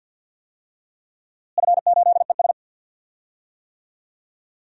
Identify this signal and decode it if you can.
Morse code, heard as a keyed tone, about 47 wpm, 700 Hz. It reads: F8ES
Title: DXCC-100 Call Signs - 4 Characters - Repeated 2x - 50wpm